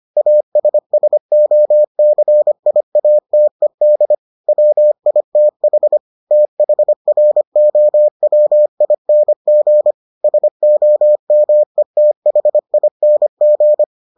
Morse code, heard as a keyed tone, about 25 wpm, 600 Hz. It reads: ASSOCIATED WITH THROWING SOMETHING